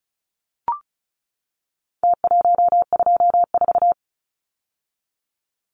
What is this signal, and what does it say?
Morse code, heard as a keyed tone, about 34 wpm, 700 Hz. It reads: T124
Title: Numbers - Contest Style - 4-Digit 35wpm